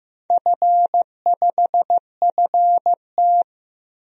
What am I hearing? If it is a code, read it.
Morse code, heard as a keyed tone, about 15 wpm, 700 Hz. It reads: F5FT
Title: DXCC-100 Call Signs - 4 Characters - Repeated 3x - 15wpm